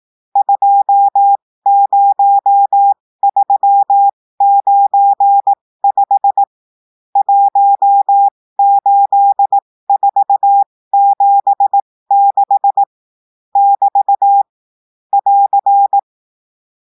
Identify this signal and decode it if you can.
Morse code, heard as a keyed tone, about 18 wpm, spaced as for 15 wpm, 800 Hz. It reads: 20395 18476 <BT> <AR>
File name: cours-du-lundi files / cours-1580-texte-chiffre-15-mpm